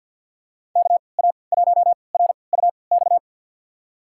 Morse code, heard as a keyed tone, about 52 wpm, spaced as for 29 wpm, 700 Hz. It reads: KA1RUX
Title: USA Call Signs - 6 Characters - 3x Character Spacing - 50wpm